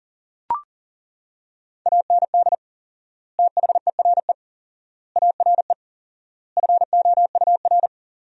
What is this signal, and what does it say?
Morse code, heard as a keyed tone, about 37 wpm, 700 Hz. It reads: AND THERE ARE FOUR
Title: 4x Word-Spacing - Sentences from Top 300 Words - 40wpm